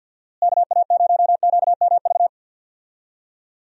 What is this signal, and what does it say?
Morse code, heard as a keyed tone, about 50 wpm, 700 Hz. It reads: KA0QMV
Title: USA Call Signs - 6 Characters - Repeated 3x - 50wpm